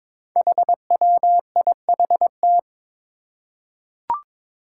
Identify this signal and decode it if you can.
Morse code, heard as a keyed tone, about 23 wpm, 700 Hz. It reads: HWIHT
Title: Five Challenging Letter-Number Combinations 22wpm